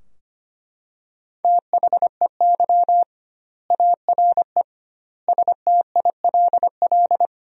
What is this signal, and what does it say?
Morse code, heard as a keyed tone, about 25 wpm, 700 Hz. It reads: THEY ARE STILL
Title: 2x Word-Spacing - Sentences from Top 400 Words - 25wpm